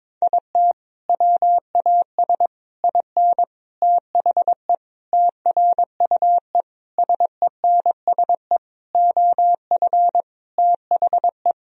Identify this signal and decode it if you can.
Morse code, heard as a keyed tone, about 22 wpm, 700 Hz. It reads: IT WAS IN THE TRUE SENSE OF THE
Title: Sentences from Top 1000 Words - Review Part 4 - 22wpm